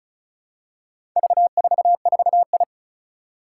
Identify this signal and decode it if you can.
Morse code, heard as a keyed tone, about 35 wpm, 690 Hz. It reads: V44I
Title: Four Challenging Letter-Number Combinations 35wpm